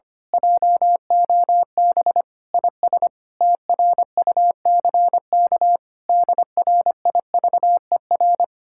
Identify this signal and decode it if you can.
Morse code, heard as a keyed tone, about 24 wpm, 700 Hz. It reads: JOB IS TRUCK DRIVER
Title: Speed-Racing Common QSO Phrases - 25wpm